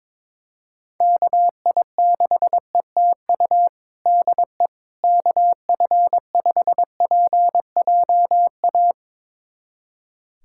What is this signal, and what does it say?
Morse code, heard as a keyed tone, about 21 wpm, 700 Hz. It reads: KI6ETU DE KF5PJA
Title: USA Callsign DE Callsign - 6 Characters - 22wpm